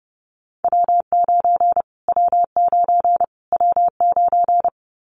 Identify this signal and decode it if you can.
Morse code, heard as a keyed tone, about 30 wpm, 700 Hz. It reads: W9 W9 W9
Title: Callsign Prefixes for Top 25 DXCC - Repeated 3x - 30wpm